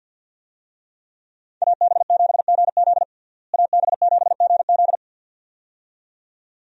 Morse code, heard as a keyed tone, about 53 wpm, 700 Hz. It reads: AB7GZ AB7GZ
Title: USA Call Signs - 5 Characters - Repeated 2x - 50wpm